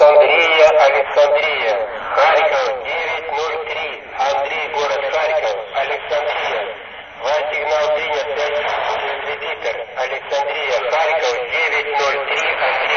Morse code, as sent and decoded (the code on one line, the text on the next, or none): none